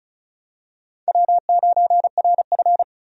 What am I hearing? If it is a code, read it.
Morse code, heard as a keyed tone, about 36 wpm, 700 Hz. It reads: W9RF
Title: USA Call Signs - 4 Characters - 35wpm